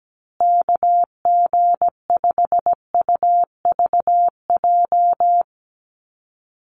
Morse code, heard as a keyed tone, about 17 wpm, 700 Hz. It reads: KG5UVJ
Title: USA Call Signs Repeated 3x 17wpm